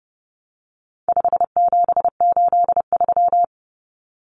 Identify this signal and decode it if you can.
Morse code, heard as a keyed tone, about 30 wpm, 700 Hz. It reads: 5783
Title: Numbers - 4-Digit 30wpm